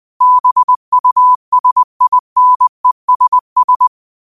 Morse code, heard as a keyed tone, about 20 wpm, 1000 Hz. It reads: BUSINESS
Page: Quotes for Sat, 16 Aug 2025 in Morse Code at 20 words per minute.